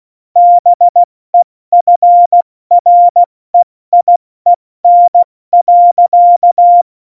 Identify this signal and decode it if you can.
Morse code, heard as a keyed tone, about 16 wpm, 700 Hz. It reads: BEFREIEN.
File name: Morsekurs / Witze / Witz 30. Juli 2019, 70 F80